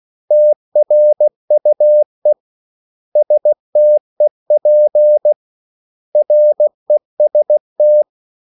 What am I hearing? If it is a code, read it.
Morse code, heard as a keyed tone, about 16 wpm, 600 Hz. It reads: TRUE STEP REST
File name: CW / Practice / 500MostCommonEnglishWords16